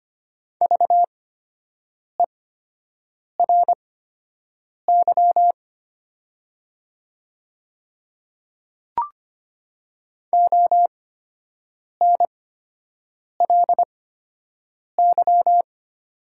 Morse code, heard as a keyed tone, about 25 wpm, spaced as for 7 wpm, 700 Hz. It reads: VERY ONLY
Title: 8x Character Spacing - Top 500 Words - 25wpm